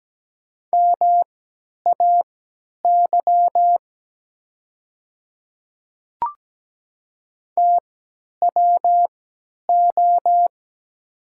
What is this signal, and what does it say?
Morse code, heard as a keyed tone, about 17 wpm, spaced as for 10 wpm, 700 Hz. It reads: MAY TWO
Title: 3x Character Spacing - Top 100 Words - 17wpm